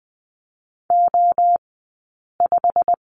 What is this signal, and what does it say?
Morse code, heard as a keyed tone, about 20 wpm, spaced as for 8 wpm, 700 Hz. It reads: O5
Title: 197 Lesson 32d - Letter k - Two Characters 20wpm